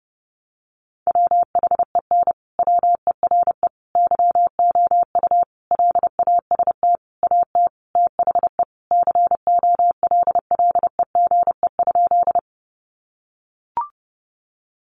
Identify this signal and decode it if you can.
Morse code, heard as a keyed tone, about 30 wpm, 700 Hz. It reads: WHEN WERE YOU LAST AT THE COLLEGE?
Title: Sentences from Top 1300 Words 30wpm